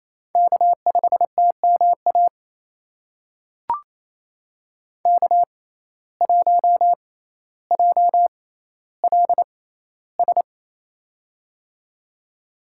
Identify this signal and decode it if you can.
Morse code, heard as a keyed tone, about 28 wpm, spaced as for 14 wpm, 700 Hz. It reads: K5TMA K1JLS
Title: USA Call Signs - 5 Characters - 6x Character Spacing - 28wpm